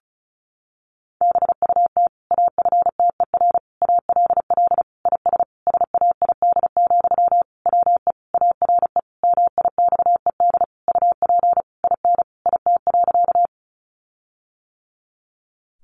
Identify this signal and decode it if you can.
Morse code, heard as a keyed tone, about 34 wpm, 700 Hz. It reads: BUT AFTER ALL IS SAID, WE ARE MIXED UP IN IT.
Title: Sentences from Top 1500 Words 35wpm